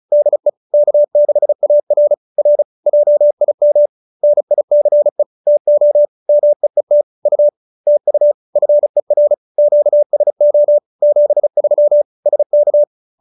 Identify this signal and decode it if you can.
Morse code, heard as a keyed tone, about 35 wpm, 590 Hz. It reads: DE K6AR R JIM NICE TO MEET U TU FER QSO 73 SK